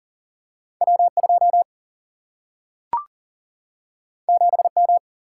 Morse code, heard as a keyed tone, about 38 wpm, 700 Hz. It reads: W 2 7 M
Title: Single-Letter Single-Number 40wpm